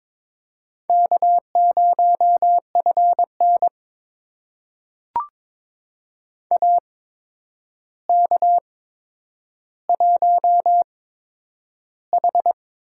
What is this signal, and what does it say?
Morse code, heard as a keyed tone, about 22 wpm, spaced as for 9 wpm, 700 Hz. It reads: K0FN AK1H
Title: USA Call Signs - 4 Characters - 8x Character Spacing - 22wpm